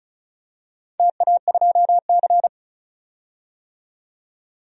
Morse code, heard as a keyed tone, about 36 wpm, 700 Hz. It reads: TA2C
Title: DXCC-100 Call Signs - 4 Characters - 35wpm